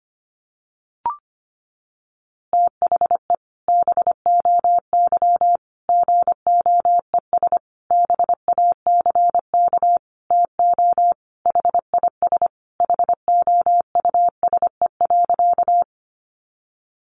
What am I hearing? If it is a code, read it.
Morse code, heard as a keyed tone, about 25 wpm, 700 Hz. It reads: THE BOY GOES BACK TO HIS HOUSE.
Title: Easy Phrases 25wpm